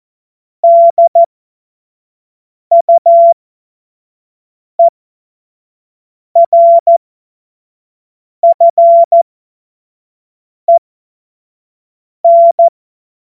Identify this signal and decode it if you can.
Morse code, heard as a keyed tone, about 14 wpm, spaced as for 5 wpm, 680 Hz. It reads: DUERFEN
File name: Morsekurs / Witze / Witz 22. November 2022 25 F70